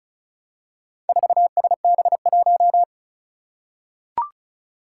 Morse code, heard as a keyed tone, about 36 wpm, 700 Hz. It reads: 4SB1
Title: Four Challenging Letter-Number Combinations 35wpm